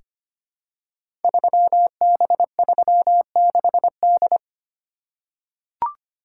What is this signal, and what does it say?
Morse code, heard as a keyed tone, about 25 wpm, 700 Hz. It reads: 3B36D
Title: Code Groups - 5 Letter:Numbers - 25wpm